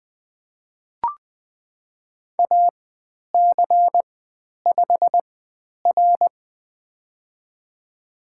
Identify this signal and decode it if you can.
Morse code, heard as a keyed tone, about 20 wpm, spaced as for 10 wpm, 700 Hz. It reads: AC5R
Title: 095 Lesson 16g - Letter c - Call Signs 20wpm